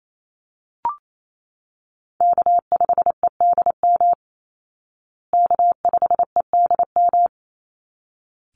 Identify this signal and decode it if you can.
Morse code, heard as a keyed tone, about 28 wpm, 700 Hz. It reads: K5EDM K5EDM
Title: Call Signs Repeated 3x - 28wpm